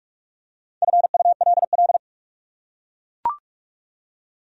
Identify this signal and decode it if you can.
Morse code, heard as a keyed tone, about 44 wpm, 700 Hz. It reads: FULL